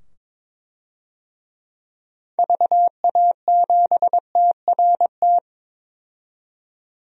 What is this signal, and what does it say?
Morse code, heard as a keyed tone, about 22 wpm, 700 Hz. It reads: VA7TRT
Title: DXCC-100 Call Signs - 6 Characters - 22wpm